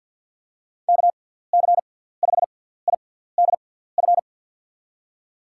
Morse code, heard as a keyed tone, about 51 wpm, spaced as for 17 wpm, 700 Hz. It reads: KC5IDF
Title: USA Call Signs - 6 Characters - 6x Character Spacing - 50wpm